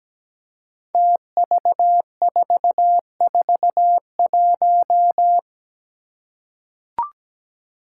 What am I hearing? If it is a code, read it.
Morse code, heard as a keyed tone, about 17 wpm, 700 Hz. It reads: TV441
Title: Five Challenging Letter-Number Combinations 17wpm